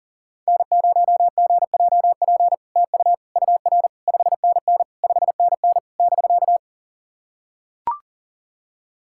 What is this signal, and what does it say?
Morse code, heard as a keyed tone, about 40 wpm, 690 Hz. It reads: N0GJP TU UR 5NN 5NN <BK>
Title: SOTA QSO Practice 40wpm